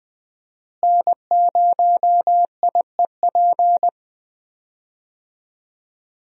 Morse code, heard as a keyed tone, about 20 wpm, 700 Hz. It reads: N0IEP
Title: USA Call Signs - 5 Characters - Repeated 3x - 20wpm